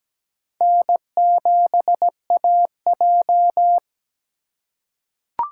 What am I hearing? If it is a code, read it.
Morse code, heard as a keyed tone, about 17 wpm, 700 Hz. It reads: N7AJ
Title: USA Call Signs - 4 Characters - 3x Character Spacing - 17wpm